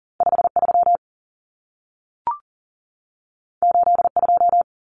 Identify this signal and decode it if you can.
Morse code, heard as a keyed tone, about 40 wpm, 700 Hz. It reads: 53 82
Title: Common Cut Numbers - 2-Digit 40wpm